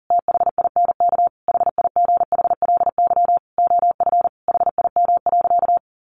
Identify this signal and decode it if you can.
Morse code, heard as a keyed tone, about 40 wpm, 700 Hz. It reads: THINK HIGHLY OF HIM.
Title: Sentences from Top 1100 Words 40wpm